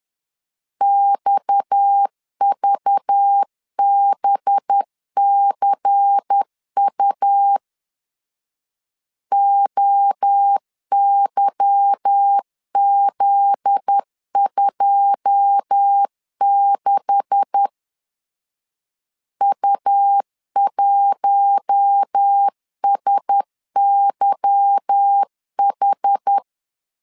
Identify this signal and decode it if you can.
Morse code, heard as a keyed tone, about 10 wpm, 800 Hz. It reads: XVBCU OYZ26 U1SYH
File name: LEZIONI / 9 2867 10wpm 2733